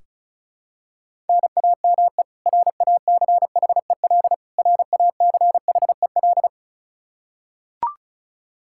Morse code, heard as a keyed tone, about 36 wpm, 700 Hz. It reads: NAME RACHEL RACHEL
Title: Practice QSO - Exchange 1 - Beginning - 35wpm